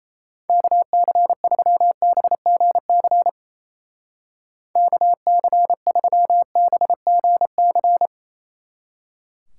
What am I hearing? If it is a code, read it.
Morse code, heard as a keyed tone, about 36 wpm, 700 Hz. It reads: KC3BGC KC3BGC